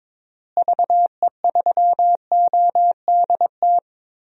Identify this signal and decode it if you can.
Morse code, heard as a keyed tone, about 22 wpm, 700 Hz. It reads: VE3ODT
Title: DXCC-100 Call Signs - 6 Characters - Repeated 3x - 22wpm